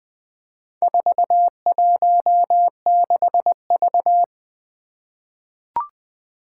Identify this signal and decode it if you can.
Morse code, heard as a keyed tone, about 20 wpm, 700 Hz. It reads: 416V
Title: Four Challenging Letter-Number Combinations 20wpm